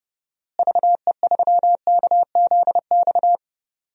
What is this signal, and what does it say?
Morse code, heard as a keyed tone, about 30 wpm, 700 Hz. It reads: VE3KZX
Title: DXCC-100 Call Signs - 6 Characters - Repeated 3x - 30wpm